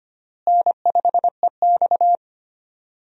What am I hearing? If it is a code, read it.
Morse code, heard as a keyed tone, about 25 wpm, 700 Hz. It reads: N5EX
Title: USA Call Signs - 4 Characters - Repeated 2x - 25wpm